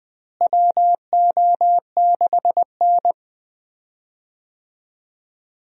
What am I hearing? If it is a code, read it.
Morse code, heard as a keyed tone, about 21 wpm, 700 Hz. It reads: WO6N
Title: USA Call Signs - 4 Characters - Repeated 2x - 20wpm